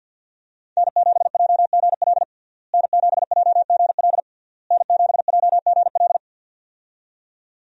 Morse code, heard as a keyed tone, about 52 wpm, 700 Hz. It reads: N7JGL N7JGL N7JGL
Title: USA Call Signs - 5 Characters - Repeated 3x - 50wpm